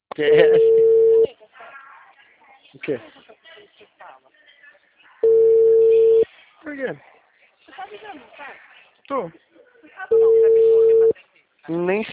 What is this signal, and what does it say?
Ringback tone (China, 450 Hz), with speech.
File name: out- -2012